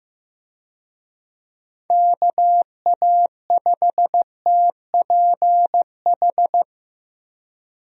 Morse code, heard as a keyed tone, about 15 wpm, 700 Hz. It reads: KA5TPH
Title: USA Call Signs - 6 Characters - 15wpm